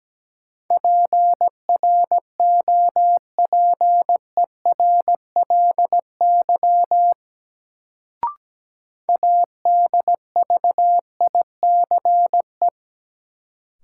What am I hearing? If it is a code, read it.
Morse code, heard as a keyed tone, about 17 wpm, 700 Hz. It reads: PROPERLY ADVICE
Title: Sentences from Top 1100 Words - New Word Drill - 17wpm